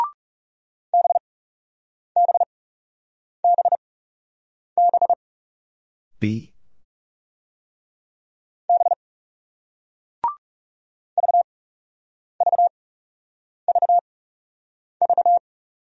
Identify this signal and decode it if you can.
Morse code, heard as a keyed tone, about 37 wpm, spaced as for 8 wpm, 700 Hz. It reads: BBBB B VVVV
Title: Speed-Racing Single Letter-Number 30wpm